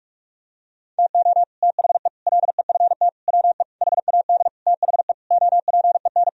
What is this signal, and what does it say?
Morse code, heard as a keyed tone, about 41 wpm, 700 Hz. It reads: TO THE LEFT WE HAD THE OPEN
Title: Sentences from Top 1000 Words - Review Part 1 - 45wpm